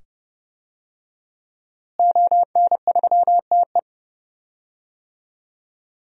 Morse code, heard as a keyed tone, about 30 wpm, 700 Hz. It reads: ON3TE
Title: DXCC-100 Call Signs - 5 Characters - Repeated 2x - 30wpm